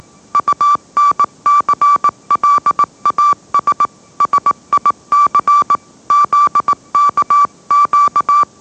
Morse code, heard as a keyed tone, about 18 wpm, 1160 Hz. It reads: UNCLAS SIC ZKQ